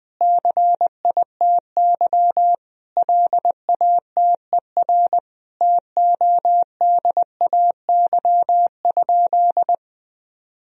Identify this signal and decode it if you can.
Morse code, heard as a keyed tone, about 20 wpm, 700 Hz. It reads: CITY LATER TODAY?
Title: Sentences from Top 1000 Words - Review Part 2 - 20wpm